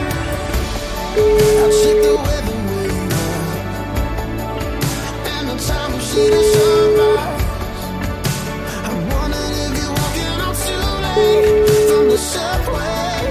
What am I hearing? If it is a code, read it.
Ringback tone (Europe, 425 Hz), with noise.